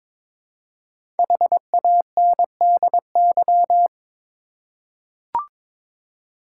Morse code, heard as a keyed tone, about 22 wpm, 700 Hz. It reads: HANDY
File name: 5 Letter Words - First 3 Letters Given - 22wpm